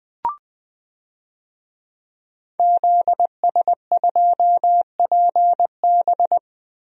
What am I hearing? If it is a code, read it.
Morse code, heard as a keyed tone, about 20 wpm, 700 Hz. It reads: ZS2PB
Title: DXCC-100 Call Signs - 5 Characters - Repeated 2x - 20wpm